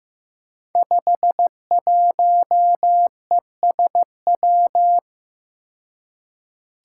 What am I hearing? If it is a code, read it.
Morse code, heard as a keyed tone, about 15 wpm, 700 Hz. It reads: 51ESW
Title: Five Challenging Letter-Number Combinations 15wpm